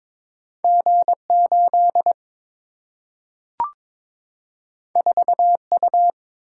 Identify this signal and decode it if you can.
Morse code, heard as a keyed tone, about 22 wpm, 700 Hz. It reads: G8 4U
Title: Single-Letter Single-Number 22wpm